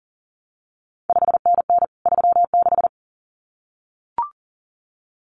Morse code, heard as a keyed tone, about 40 wpm, 700 Hz. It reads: 5NN 36
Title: CQ WW DX Contest Exchange Encore - 40wpm